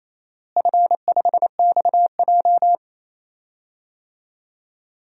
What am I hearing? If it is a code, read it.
Morse code, heard as a keyed tone, about 28 wpm, 700 Hz. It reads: F5XJ
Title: DXCC-100 Call Signs - 4 Characters - Repeated 2x - 28wpm